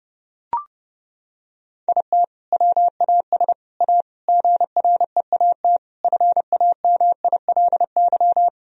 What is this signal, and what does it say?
Morse code, heard as a keyed tone, about 30 wpm, 700 Hz. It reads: IT WAS A GREAT FAMILY
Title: Sentences from Top 1000 Words - Review Part 2 - 30wpm